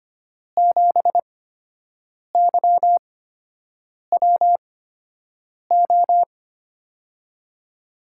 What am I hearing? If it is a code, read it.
Morse code, heard as a keyed tone, about 25 wpm, spaced as for 7 wpm, 700 Hz. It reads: 7YWO
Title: USA Call Signs - 8x Character Spacing - 25wpm